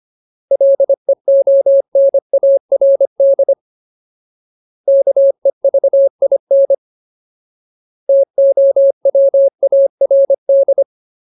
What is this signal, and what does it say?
Morse code, heard as a keyed tone, about 25 wpm, 550 Hz. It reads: LEONARD KEVIN TOWARD
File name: Ultimate Word List / 25WPM Ultimate